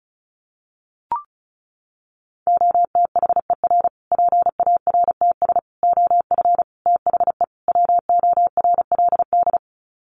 Morse code, heard as a keyed tone, about 34 wpm, 700 Hz. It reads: OTHER PARTS OF THE WORLD
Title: Sets of 5 Words Encore 35wpm